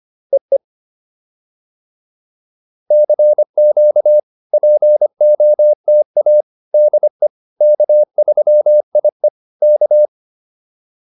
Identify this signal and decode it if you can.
Morse code, heard as a keyed tone, about 25 wpm, 600 Hz. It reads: EE CQ POTA DE K3IE K
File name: POTA307a 25